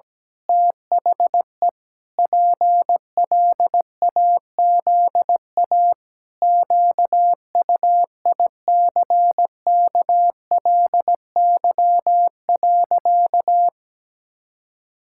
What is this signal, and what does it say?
Morse code, heard as a keyed tone, about 17 wpm, 700 Hz. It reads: THE PLAZA QUICKLY.
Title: Pangram Challenge v2 - 17wpm